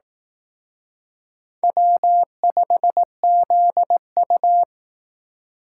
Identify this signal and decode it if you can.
Morse code, heard as a keyed tone, about 19 wpm, 700 Hz. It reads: W5ZU